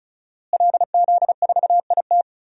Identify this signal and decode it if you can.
Morse code, heard as a keyed tone, about 36 wpm, 700 Hz. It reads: LZ4IT
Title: DXCC-100 Call Signs - 5 Characters - Repeated 2x - 35wpm